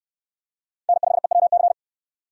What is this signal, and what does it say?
Morse code, heard as a keyed tone, about 69 wpm, 700 Hz. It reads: N5EWC